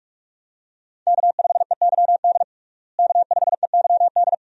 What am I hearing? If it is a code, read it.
Morse code, heard as a keyed tone, about 46 wpm, 700 Hz. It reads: K5EYD K5EYD
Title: USA Call Signs - 5 Characters - Repeated 2x - 45wpm